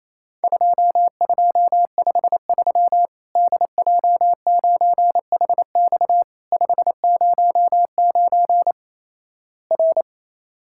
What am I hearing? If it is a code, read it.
Morse code, heard as a keyed tone, about 28 wpm, 700 Hz. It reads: 2253 DJ9HX 509 R
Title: Worked All Europe DX Contest - QTC Exchange - 28wpm